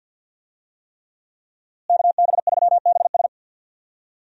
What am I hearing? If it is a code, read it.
Morse code, heard as a keyed tone, about 52 wpm, 700 Hz. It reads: KB3BS
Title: USA Call Signs - 5 Characters - Repeated 3x - 50wpm